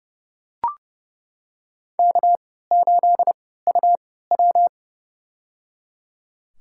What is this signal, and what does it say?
Morse code, heard as a keyed tone, about 30 wpm, spaced as for 17 wpm, 700 Hz. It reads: K8UW
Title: USA Call Signs - 3x Character Spacing - 30wpm